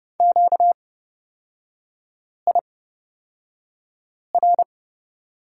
Morse code, heard as a keyed tone, about 29 wpm, spaced as for 5 wpm, 700 Hz. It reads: QIR
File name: Lesson 38e - Letter q - Three Characters 30wpm